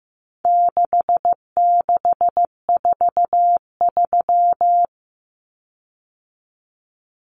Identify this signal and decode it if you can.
Morse code, heard as a keyed tone, about 15 wpm, 700 Hz. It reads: 6643
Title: Common Cut Numbers - 4-Digit 15wpm